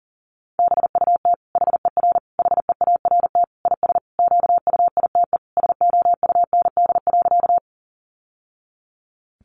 Morse code, heard as a keyed tone, about 40 wpm, 700 Hz. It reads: BUT HER HEART IS QUITE SOUND.
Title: Sentences from Top 1200 Words 40wpm